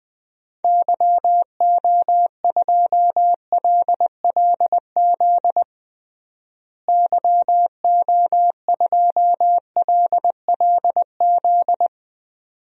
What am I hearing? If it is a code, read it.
Morse code, heard as a keyed tone, about 21 wpm, 700 Hz. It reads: YO2LLZ YO2LLZ
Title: DXCC-100 Call Signs - 6 Characters - Repeated 2x - 20wpm